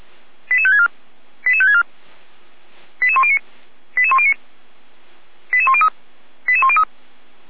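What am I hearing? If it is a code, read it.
Five-tone selective call (ZVEI), heard as quick runs of five tones, tones of about 80 ms. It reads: 88564 88564 88109 88109 88103 88103
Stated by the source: Received on SDR as a ZVEI five-tone selective call, address 88190.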